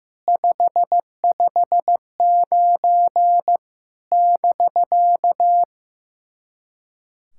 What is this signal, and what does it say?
Morse code, heard as a keyed tone, about 15 wpm, 700 Hz. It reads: 559 <BK>
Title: SOTA QSO Practice 15wpm